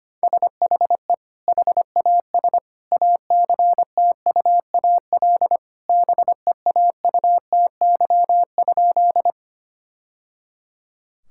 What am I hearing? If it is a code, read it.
Morse code, heard as a keyed tone, about 25 wpm, 700 Hz. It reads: SHE HAS ACTUAL BEAUTY?
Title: Sentences from Top 1000 Words - Review Part 5 - 25wpm